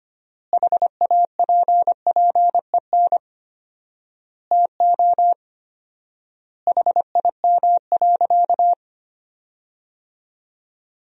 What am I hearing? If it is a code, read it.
Morse code, heard as a keyed tone, about 25 wpm, 700 Hz. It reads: HAPPEN TO HIM.
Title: 4x Word-Spacing - Sentences from Top 400 Words - 25wpm